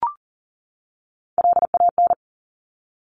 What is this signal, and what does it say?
Morse code, heard as a keyed tone, about 38 wpm, 700 Hz. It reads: LAN
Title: Top Three Letter Combinations in Words 40wpm